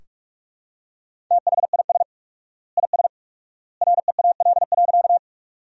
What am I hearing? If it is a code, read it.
Morse code, heard as a keyed tone, about 43 wpm, 700 Hz. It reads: THIS IS REAL.
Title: 4x Word-Spacing - Sentences from Top 300 Words - 45wpm